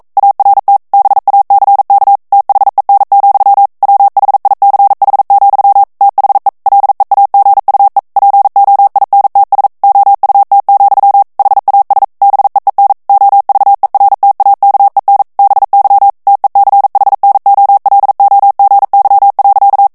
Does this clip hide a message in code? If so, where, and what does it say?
Yes: Morse code from 0.2 to 19.9 s, heard as a keyed tone, about 42 wpm, 800 Hz. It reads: ART BACK THEN, WHICH, THE LEAGUE POINTS OUT, HAS BEEN OVERTAKEN BY TECHNOLOGY.